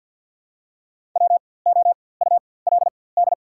Morse code, heard as a keyed tone, about 53 wpm, spaced as for 24 wpm, 690 Hz. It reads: WOULD
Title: 4x Character Spacing - Top 100 Words - 50wpm